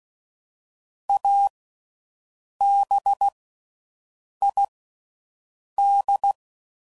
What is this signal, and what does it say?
Morse code, heard as a keyed tone, about 16 wpm, 800 Hz. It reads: A B I D
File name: Slow With Faster Ings